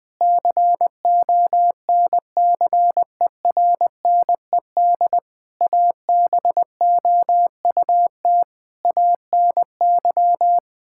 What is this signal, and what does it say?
Morse code, heard as a keyed tone, about 20 wpm, 700 Hz. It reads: CONCERNED ABOUT ANY
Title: Sentences from Top 1000 Words - Review Part 6 - 20wpm